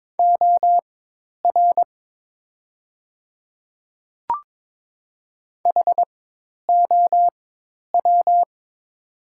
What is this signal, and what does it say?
Morse code, heard as a keyed tone, about 22 wpm, spaced as for 10 wpm, 700 Hz. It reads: OR HOW
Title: 4x Character Spacing - Top 300 Words - 22wpm